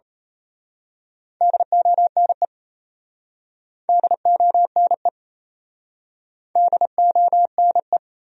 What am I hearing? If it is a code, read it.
Morse code, heard as a keyed tone, about 32 wpm, 700 Hz. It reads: DONE DONE DONE
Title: Speed-Racing Top 500 Words 28wpm